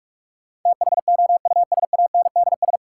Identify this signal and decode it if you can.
Morse code, heard as a keyed tone, about 42 wpm, 700 Hz. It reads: THOUSANDS
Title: Sets of 4 Words - New Word Drill - 45wpm